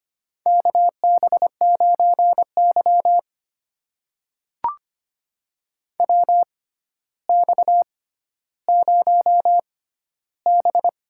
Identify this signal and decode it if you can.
Morse code, heard as a keyed tone, about 25 wpm, spaced as for 13 wpm, 700 Hz. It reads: KB9Y WX0B
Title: USA Call Signs - 6x Character Spacing - 25wpm